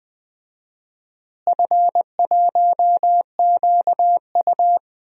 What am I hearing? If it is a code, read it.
Morse code, heard as a keyed tone, about 21 wpm, 700 Hz. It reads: F1QU
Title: DXCC-100 Call Signs - 4 Characters - Repeated 2x - 20wpm